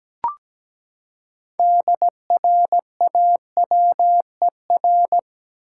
Morse code, heard as a keyed tone, about 17 wpm, 700 Hz. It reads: DRAWER
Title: Six Letter Words - 17wpm